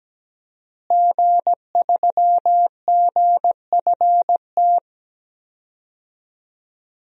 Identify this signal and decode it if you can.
Morse code, heard as a keyed tone, about 17 wpm, 700 Hz. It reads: G3GFT
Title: DXCC-100 Call Signs - 5 Characters - Repeated 2x - 17wpm